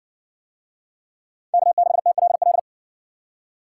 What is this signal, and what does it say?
Morse code, heard as a keyed tone, about 55 wpm, 700 Hz. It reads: K6TLL